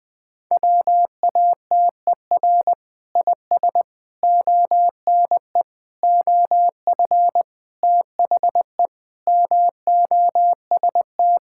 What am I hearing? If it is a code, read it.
Morse code, heard as a keyed tone, about 20 wpm, 700 Hz. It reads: WATER IS ONE OF THE MOST
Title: Sentences with 5-Word Phrases - Spoken Beginning - 20wpm